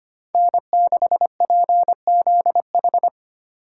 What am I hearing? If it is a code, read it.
Morse code, heard as a keyed tone, about 32 wpm, 700 Hz. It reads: N6PZH